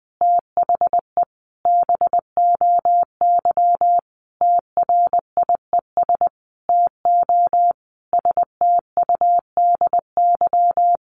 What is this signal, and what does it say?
Morse code, heard as a keyed tone, about 20 wpm, 700 Hz. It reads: THE BOY TRIES TO STUDY
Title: Additional Easy Phrases 20wpm